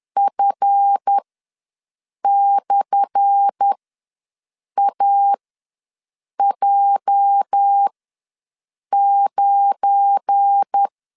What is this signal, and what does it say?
Morse code, heard as a keyed tone, about 10 wpm, spaced as for 6 wpm, 800 Hz. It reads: F/AJ9
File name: LEZIONI / 10 SIGNS / 10C DOT ws 10wpm 4213